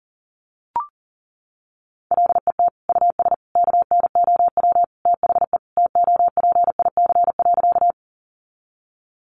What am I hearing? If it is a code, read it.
Morse code, heard as a keyed tone, about 40 wpm, 700 Hz. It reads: LET US KNOW THE TOPIC.
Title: Sentences from Top 1200 Words 40wpm